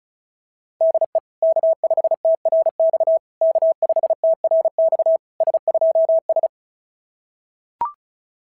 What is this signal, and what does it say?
Morse code, heard as a keyed tone, about 36 wpm, 660 Hz. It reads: DE K5TRX K5TRX S2S
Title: SOTA QSO Practice 35wpm